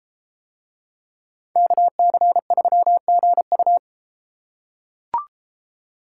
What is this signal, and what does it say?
Morse code, heard as a keyed tone, about 31 wpm, 700 Hz. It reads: KC3GU